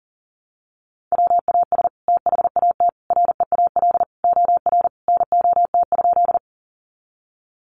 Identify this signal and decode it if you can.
Morse code, heard as a keyed tone, about 37 wpm, 700 Hz. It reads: WAS THAT REAL OR NOT?
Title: Sentences from Top 600 Words 40wpm